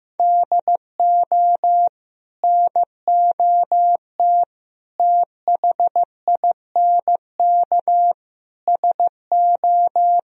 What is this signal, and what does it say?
Morse code, heard as a keyed tone, about 16 wpm, 700 Hz. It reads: DO NOT THINK SO
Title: Speed-Racing - Sets of 4 Words 15wpm